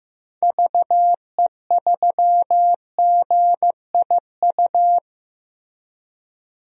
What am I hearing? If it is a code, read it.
Morse code, heard as a keyed tone, about 15 wpm, 700 Hz. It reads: VE3GIU
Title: DXCC-100 Call Signs - 6 Characters - Repeated 3x - 15wpm